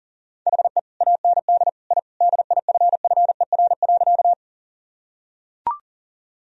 Morse code, heard as a keyed tone, about 40 wpm, 700 Hz. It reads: HE AND I DIFFER.
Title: Sentences from Top 1000 Words - Review Part 1 - 40wpm